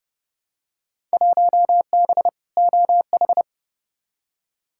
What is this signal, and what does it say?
Morse code, heard as a keyed tone, about 30 wpm, 700 Hz. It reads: 1B OH
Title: Field Day Exchanges - 30wpm